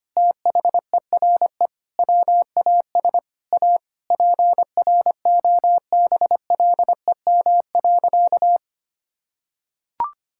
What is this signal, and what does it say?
Morse code, heard as a keyed tone, about 25 wpm, 700 Hz. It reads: THERE WAS A PROBLEM.
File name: practice / 25wpm / Sentences with 4-Word Phrases - 25wpm